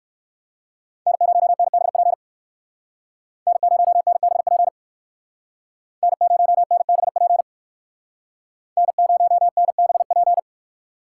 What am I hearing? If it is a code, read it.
Morse code, heard as a keyed tone, about 58 wpm, 700 Hz. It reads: N0NBP N0NBP N0NBP N0NBP